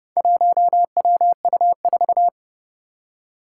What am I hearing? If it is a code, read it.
Morse code, heard as a keyed tone, about 30 wpm, 700 Hz. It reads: 1WU4
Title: Four Challenging Letter-Number Combinations 30wpm